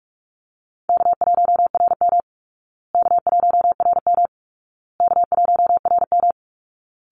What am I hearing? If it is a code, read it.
Morse code, heard as a keyed tone, about 44 wpm, 700 Hz. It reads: K1RM K1RM K1RM
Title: Easy Call Signs Repeated 3x - 45wpm